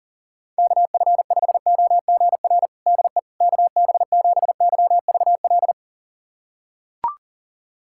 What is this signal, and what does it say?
Morse code, heard as a keyed tone, about 40 wpm, 700 Hz. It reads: KF5OGR DE KB7YVL
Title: USA Callsign DE Callsign - 6 Characters - 40wpm